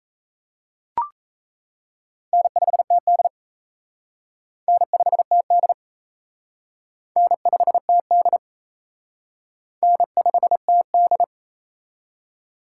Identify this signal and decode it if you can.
Morse code, heard as a keyed tone, about 35 wpm, 700 Hz. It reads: N5TD N5TD N5TD N5TD
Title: Speed Racing - USA 1x2 Call Signs 28wpm